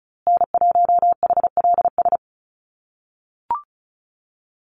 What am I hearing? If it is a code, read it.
Morse code, heard as a keyed tone, about 34 wpm, 700 Hz. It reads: N1HLS
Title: USA Call Signs Repeated 3x 35wpm